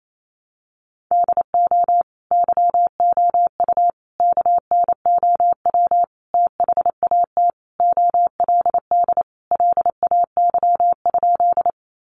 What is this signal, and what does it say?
Morse code, heard as a keyed tone, about 28 wpm, 700 Hz. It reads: DO YOU KNOW THAT OLD LAY?
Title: Sentences from Top 400 Words 28wpm